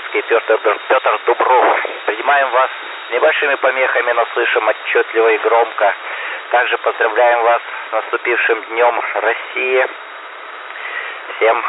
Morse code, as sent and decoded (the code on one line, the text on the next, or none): none